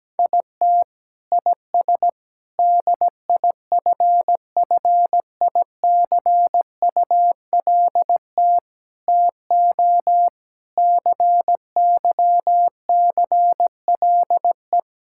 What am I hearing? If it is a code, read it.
Morse code, heard as a keyed tone, about 17 wpm, 700 Hz. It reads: IT IS DIFFICULT TO CYCLE
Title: Sentences from Top 1000 Words - Review Part 6 - 17wpm